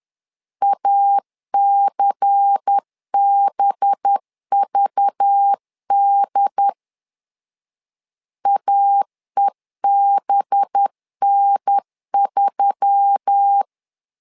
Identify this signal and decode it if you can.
Morse code, heard as a keyed tone, about 10 wpm, 800 Hz. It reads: ACBVD AEBN3
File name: LEZIONI / 5 LBVC / 5 LBVC 10wpm 2349